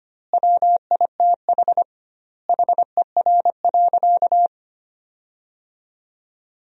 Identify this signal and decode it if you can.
Morse code, heard as a keyed tone, about 25 wpm, 700 Hz. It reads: WITH HER.
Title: 2x Word-Spacing - Sentences from Top 100 Words - 25wpm